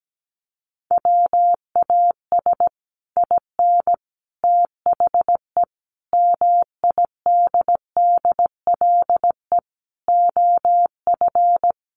Morse code, heard as a keyed tone, about 17 wpm, 700 Hz. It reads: WAS IN THE MIDDLE OF
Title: Sets of 5 Words Encore 17wpm